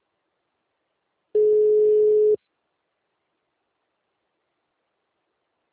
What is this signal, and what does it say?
Ringback tone (Europe, 425 Hz).